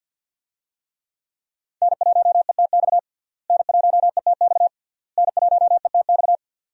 Morse code, heard as a keyed tone, about 51 wpm, 700 Hz. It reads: N1ETX N1ETX N1ETX
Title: USA Call Signs - 5 Characters - Repeated 3x - 50wpm